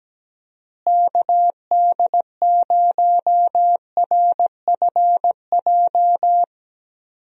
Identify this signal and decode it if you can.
Morse code, heard as a keyed tone, about 17 wpm, 700 Hz. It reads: KD0RFJ
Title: USA Call Signs - 6 Characters - Repeated 3x - 17wpm